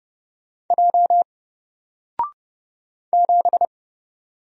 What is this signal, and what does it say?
Morse code, heard as a keyed tone, about 30 wpm, 700 Hz. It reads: J 7
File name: Single-Letter-Number-30wpm